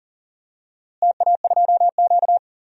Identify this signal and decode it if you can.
Morse code, heard as a keyed tone, about 40 wpm, 700 Hz. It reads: TA2Q
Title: DXCC-100 Call Signs - 4 Characters - Repeated 2x - 40wpm